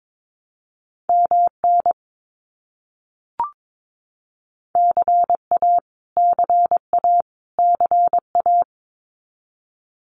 Morse code, heard as a keyed tone, about 22 wpm, 700 Hz. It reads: MN CA CA CA
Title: US States Repeated 3x 22wpm